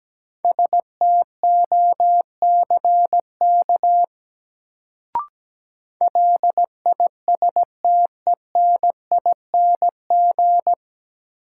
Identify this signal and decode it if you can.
Morse code, heard as a keyed tone, about 17 wpm, 700 Hz. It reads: STOCK LISTENING
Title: Sets of 5 Words - New Word Drill - 17wpm